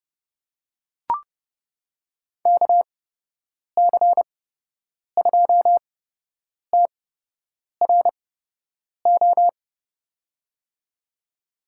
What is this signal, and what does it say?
Morse code, heard as a keyed tone, about 30 wpm, spaced as for 8 wpm, 700 Hz. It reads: KC2TRO
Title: USA Call Signs - 6 Characters - 8x Character Spacing - 30wpm